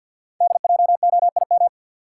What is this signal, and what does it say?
Morse code, heard as a keyed tone, about 50 wpm, 690 Hz. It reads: DJOIM
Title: Code Groups - 5 Letters - 50wpm